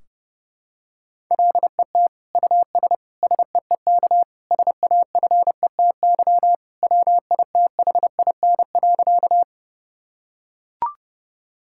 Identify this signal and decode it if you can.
Morse code, heard as a keyed tone, about 30 wpm, 700 Hz. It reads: LET US SEEK SAFETY WITHIN.
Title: Sentences from Top 1600 Words 30wpm